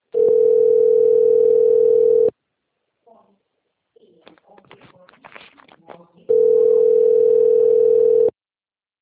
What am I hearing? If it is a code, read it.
Ringback tone (North America), 440 + 480 Hz.
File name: rg-1600-unknown- - - .31216